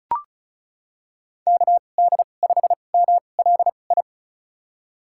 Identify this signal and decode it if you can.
Morse code, heard as a keyed tone, about 34 wpm, spaced as for 25 wpm, 700 Hz. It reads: KD5MLI
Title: USA Call Signs - 6 Characters - 2x Character Spacing - 35wpm